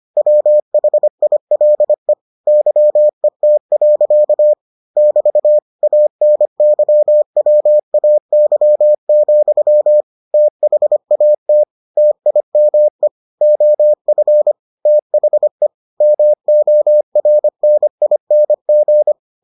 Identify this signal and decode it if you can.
Morse code, heard as a keyed tone, about 25 wpm, 600 Hz. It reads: WHILE YET. <BT> ANYWAY, THAT TIME OF THE MORNING